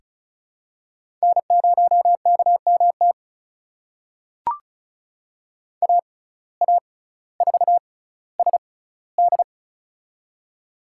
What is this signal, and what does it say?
Morse code, heard as a keyed tone, about 36 wpm, spaced as for 18 wpm, 700 Hz. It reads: N0KMT AA4SD
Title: USA Call Signs - 5 Characters - 6x Character Spacing - 35wpm